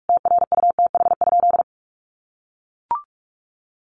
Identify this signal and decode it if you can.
Morse code, heard as a keyed tone, about 44 wpm, 700 Hz. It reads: TRUTH?
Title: Sentences from Top 1300 Words 45wpm